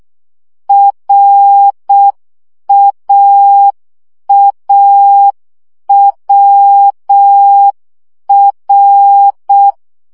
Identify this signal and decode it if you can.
Morse code, heard as a keyed tone, about 6 wpm, 800 Hz. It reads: RAAWR
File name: CW gyakorlas a r w 5 baud